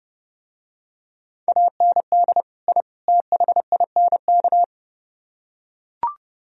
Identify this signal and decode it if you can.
Morse code, heard as a keyed tone, about 30 wpm, 700 Hz. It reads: AND I THINK